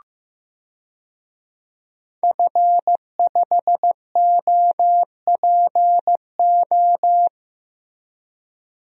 Morse code, heard as a keyed tone, about 16 wpm, 700 Hz. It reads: F5OPO
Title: DXCC-100 Call Signs - 5 Characters - Repeated 3x - 15wpm